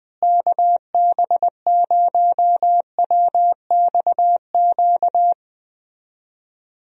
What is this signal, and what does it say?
Morse code, heard as a keyed tone, about 20 wpm, 700 Hz. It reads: KB0WXQ
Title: USA Call Signs - 6 Characters - Repeated 2x - 20wpm